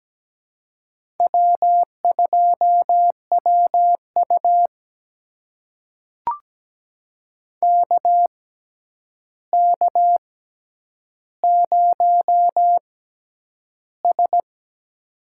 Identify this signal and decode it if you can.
Morse code, heard as a keyed tone, about 17 wpm, spaced as for 9 wpm, 700 Hz. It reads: W2WU KK0S
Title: USA Call Signs - 4 Characters - 6x Character Spacing - 17wpm